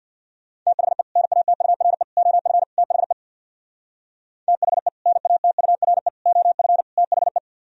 Morse code, heard as a keyed tone, about 60 wpm, 700 Hz. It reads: THE NATURE OF THE THE NATURE OF THE
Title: Speed-Racing - Sets of 4 Words 50wpm